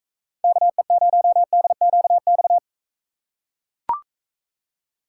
Morse code, heard as a keyed tone, about 43 wpm, 700 Hz. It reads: KE0DQX